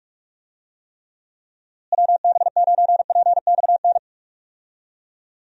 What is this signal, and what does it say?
Morse code, heard as a keyed tone, about 41 wpm, 700 Hz. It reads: WB9PXN
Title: USA Call Signs - 6 Characters - Repeated 3x - 45wpm